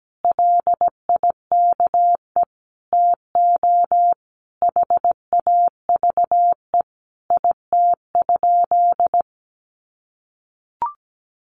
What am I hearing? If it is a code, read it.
Morse code, heard as a keyed tone, about 17 wpm, 700 Hz. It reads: LIKE TO HAVE IT?
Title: Sentences from Top 100 Words 17wpm